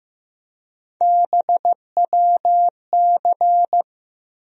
Morse code, heard as a keyed tone, about 15 wpm, 700 Hz. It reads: BWC
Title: USA Call Sign Postfixes - 3 Letters 15wpm